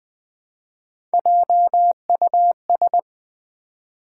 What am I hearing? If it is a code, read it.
Morse code, heard as a keyed tone, about 20 wpm, 700 Hz. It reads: JUS
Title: Build A Word - 1 Letter At A Time - 3 to 5-Letter Words - 20wpm